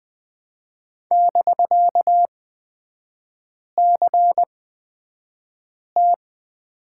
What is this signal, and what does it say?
Morse code, heard as a keyed tone, about 20 wpm, spaced as for 5 wpm, 700 Hz. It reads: <BK>CT
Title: 250 Lesson 40e - BK - Three Characters 20wpm